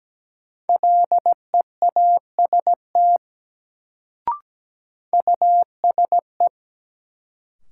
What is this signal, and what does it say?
Morse code, heard as a keyed tone, about 17 wpm, 700 Hz. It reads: LEAST USE
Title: Sets of 2 Words - New Word Drill - 17wpm